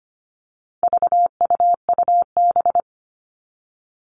Morse code, heard as a keyed tone, about 25 wpm, 700 Hz. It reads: VUUB
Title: All Cut Numbers - 4-Digit 25wpm